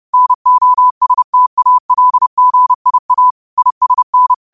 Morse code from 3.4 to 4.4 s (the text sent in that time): ISN